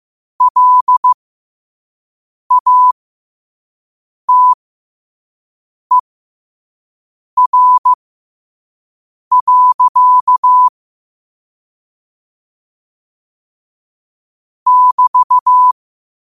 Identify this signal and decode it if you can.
Morse code, heard as a keyed tone, about 15 wpm, spaced as for 5 wpm, 1000 Hz. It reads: LATER. <BT>
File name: qotdfiles / 5wpm